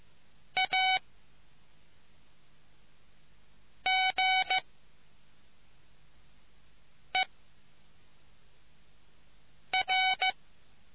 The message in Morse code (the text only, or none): A G E R